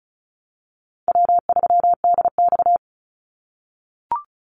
Morse code, heard as a keyed tone, about 34 wpm, 700 Hz. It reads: W3DX
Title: NA SOTA Activator Callsigns 35wpm